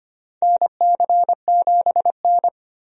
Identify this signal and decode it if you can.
Morse code, heard as a keyed tone, about 25 wpm, 700 Hz. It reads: NC7N
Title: USA Call Signs - 4 Characters - Repeated 3x - 25wpm